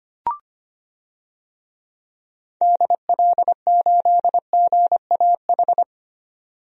Morse code, heard as a keyed tone, about 25 wpm, 700 Hz. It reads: DL8GAH
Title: DXCC-100 Call Signs - 6 Characters - 25wpm